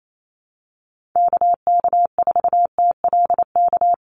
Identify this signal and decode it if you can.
Morse code, heard as a keyed tone, about 28 wpm, 700 Hz. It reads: KK4TLK
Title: USA Call Signs 28wpm